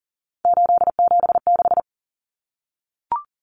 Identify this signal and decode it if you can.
Morse code, heard as a keyed tone, about 40 wpm, 700 Hz. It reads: 876
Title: Numbers - 3-Digit 40wpm